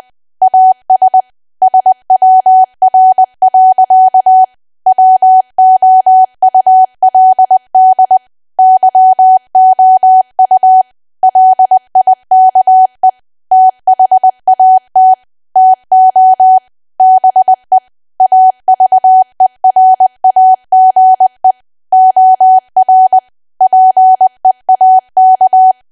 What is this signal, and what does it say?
Morse code, heard as a keyed tone, about 20 wpm, 750 Hz. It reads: AS SWR. WOULD YOU LIKE THAT TO BE AVERAGE OR PEAK
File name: cwa / W1AW-Level3 / 20WPM